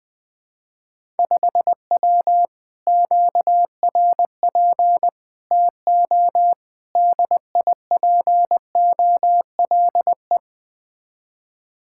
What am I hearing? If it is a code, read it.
Morse code, heard as a keyed tone, about 20 wpm, 700 Hz. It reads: 5W QRP TO DIPOLE
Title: Speed-Racing Common QSO Phrases Encore - 17wpm